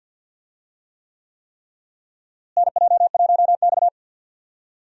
Morse code, heard as a keyed tone, about 51 wpm, 700 Hz. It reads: NJ1X
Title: USA Call Signs - 4 Characters - Repeated 3x - 50wpm